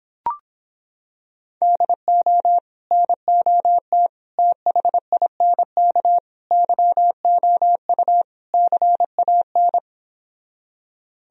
Morse code, heard as a keyed tone, about 26 wpm, 700 Hz. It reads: DO NOT THINK YOU CAN